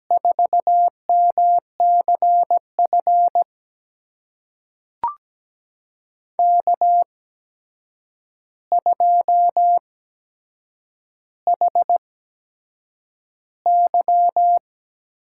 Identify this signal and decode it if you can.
Morse code, heard as a keyed tone, about 17 wpm, spaced as for 7 wpm, 700 Hz. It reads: 4MCF K2HY
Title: USA Call Signs - 5 Characters - 8x Character Spacing - 17wpm